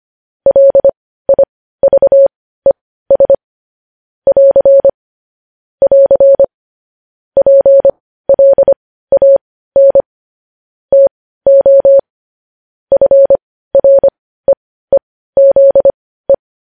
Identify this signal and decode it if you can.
Morse code, heard as a keyed tone, about 25 wpm, spaced as for 15 wpm, 550 Hz. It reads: LIVES <AR> <AR> PLAN TO FREEZE
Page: news-headlines-in-morse-code-at-15-wpm